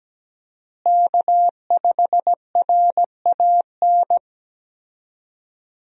Morse code, heard as a keyed tone, about 17 wpm, 700 Hz. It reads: K5RAN
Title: USA Call Signs - 5 Characters - Repeated 3x - 17wpm